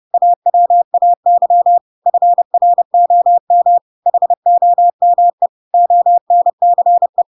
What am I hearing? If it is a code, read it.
Morse code, heard as a keyed tone, about 30 wpm, 700 Hz. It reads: AWAY FROM HOME ONCE
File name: Pinocchio / Pinocchio ch8 30 30 ik1hge